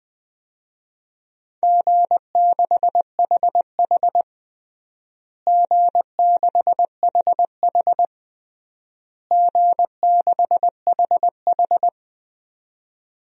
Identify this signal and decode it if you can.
Morse code, heard as a keyed tone, about 20 wpm, 700 Hz. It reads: G6HH G6HH G6HH
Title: DXCC-100 Call Signs - 4 Characters - Repeated 3x - 20wpm